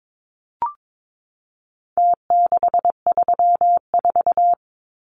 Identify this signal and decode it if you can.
Morse code, heard as a keyed tone, about 22 wpm, 700 Hz. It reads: T634
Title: Numbers - Contest Style - 4-Digit 22wpm